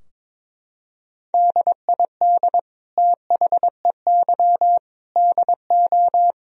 Morse code, heard as a keyed tone, about 22 wpm, 700 Hz. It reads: DID THEY DO
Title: Sentences from Top 1000 Words - Review Part 4 - 22wpm